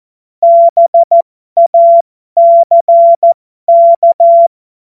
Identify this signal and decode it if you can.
Morse code, heard as a keyed tone, about 14 wpm, spaced as for 12 wpm, 680 Hz. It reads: BACK